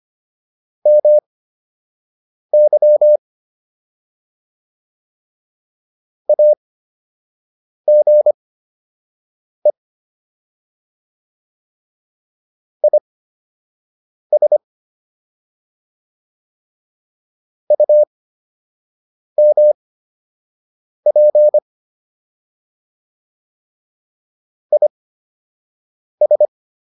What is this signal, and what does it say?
Morse code, heard as a keyed tone, about 25 wpm, spaced as for 6 wpm, 600 Hz. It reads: MY AGE IS UMP IS